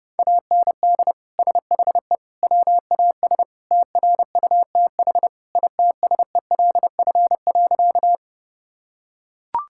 Morse code, heard as a keyed tone, about 31 wpm, 700 Hz. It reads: AND SHE WAS TRUTH ITSELF.
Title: Sentences from Top 1000 Words - Review Part 5 - 30wpm